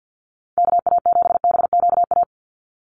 Morse code, heard as a keyed tone, about 48 wpm, 700 Hz. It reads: KA7BQA
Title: USA Call Signs 50wpm